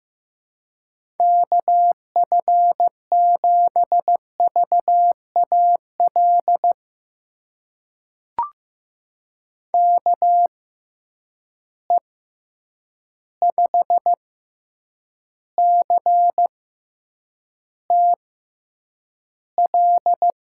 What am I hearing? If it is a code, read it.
Morse code, heard as a keyed tone, about 15 wpm, spaced as for 8 wpm, 700 Hz. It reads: KF7VAL KE5CTL
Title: USA Call Signs - 6 Characters - 6x Character Spacing - 15wpm